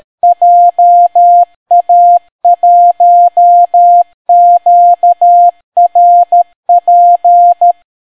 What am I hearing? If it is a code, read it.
Morse code, heard as a keyed tone, about 13 wpm, 700 Hz. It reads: JA1QRP